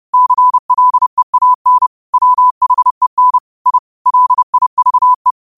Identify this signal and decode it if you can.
Morse code, heard as a keyed tone, about 30 wpm, 1000 Hz. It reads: CLEAN WHEN I LIVE
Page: Quotes for Mon, 28 Apr 2025 in Morse Code at 30 words per minute.